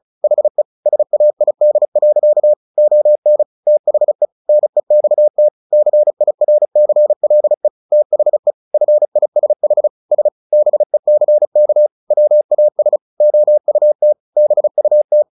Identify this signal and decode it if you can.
Morse code, heard as a keyed tone, about 34 wpm, 600 Hz. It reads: HE SAID. ON THE NEXT CIRCLE THE FISH S BECK WAS OUT BUT
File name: the old man and the sea 2 35wpm